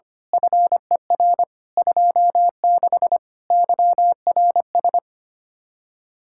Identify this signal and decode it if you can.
Morse code, heard as a keyed tone, about 24 wpm, 700 Hz. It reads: FER 26 YRS
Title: Speed-Racing Common QSO Phrases - 25wpm